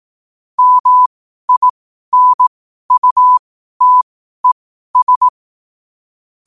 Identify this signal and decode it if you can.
Morse code, heard as a keyed tone, about 18 wpm, spaced as for 12 wpm, 1000 Hz. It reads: MINUTES
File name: qotdfiles / 12wpm